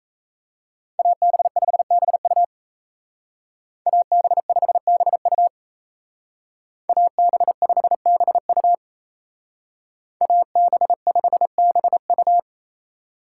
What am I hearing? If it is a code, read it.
Morse code, heard as a keyed tone, about 33 wpm, 700 Hz. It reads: AB5BU AB5BU AB5BU AB5BU
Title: Speed Racing - USA 2x2 Call Signs 28wpm